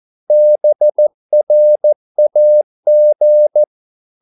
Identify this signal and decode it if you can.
Morse code, heard as a keyed tone, about 14 wpm, 600 Hz. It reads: BRAG
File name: Korte ord DK 14wpm